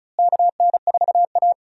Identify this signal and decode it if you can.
Morse code, heard as a keyed tone, about 36 wpm, 700 Hz. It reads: KN4A
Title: USA Call Signs - 4 Characters - Repeated 3x - 35wpm